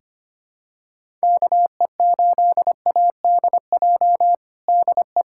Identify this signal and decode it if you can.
Morse code, heard as a keyed tone, about 25 wpm, 700 Hz. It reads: KE8ADJ DE
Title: USA Callsign DE Callsign - 6 Characters - 25wpm